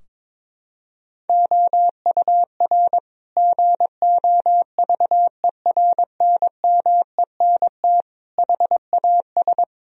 Morse code, heard as a keyed tone, about 22 wpm, 700 Hz. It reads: OUR GOVERNMENT HAS
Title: Sentences from Top 1000 Words - Review Part 4 - 22wpm